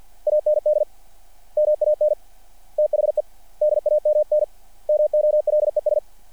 Morse code, heard as a keyed tone, about 50 wpm, 600 Hz. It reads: AND MAN THE DAMN MOLES
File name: archived / CRSnet